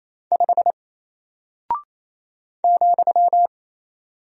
Morse code, heard as a keyed tone, about 28 wpm, 700 Hz. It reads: 5 ,
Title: Single Character (Letter:Number:Punctuation:Prosign) Alt 28wpm